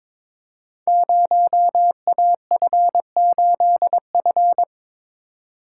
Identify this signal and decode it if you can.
Morse code, heard as a keyed tone, about 22 wpm, 700 Hz. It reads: 0AF8F
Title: Code Groups - 5 Letter:Numbers - 22wpm